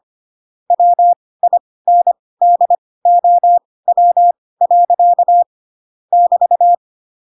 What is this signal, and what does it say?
Morse code, heard as a keyed tone, about 25 wpm, spaced as for 18 wpm, 700 Hz. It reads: WINDOW. <BT>
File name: Pinocchio / Pinocchio ch5 12 25 ik1hge